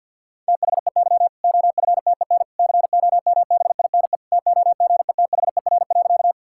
Morse code, heard as a keyed tone, about 50 wpm, 700 Hz. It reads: THEY OFTEN COMBINE TOGETHER.
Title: Sentences from Top 1600 Words 50wpm